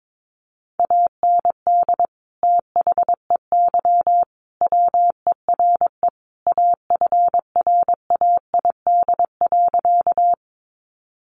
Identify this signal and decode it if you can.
Morse code, heard as a keyed tone, about 22 wpm, 700 Hz. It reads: AND THEY WERE AFRAID.
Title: Sentences from Top 1500 Words 22wpm